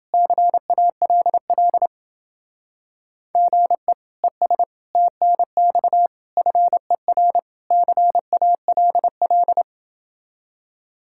Morse code, heard as a keyed tone, about 29 wpm, 700 Hz. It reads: CALL GE ES TNX FER CALL